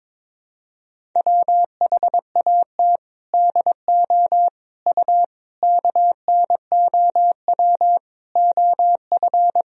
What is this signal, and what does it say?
Morse code, heard as a keyed tone, about 22 wpm, 700 Hz. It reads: WHAT DO U KNOW OF
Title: Sentences from Top 100 Words and Abbreviations 22wpm